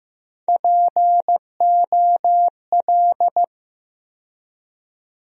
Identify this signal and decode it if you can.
Morse code, heard as a keyed tone, about 15 wpm, 700 Hz. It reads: POL
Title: USA Call Sign Postfixes - 3 Letters 15wpm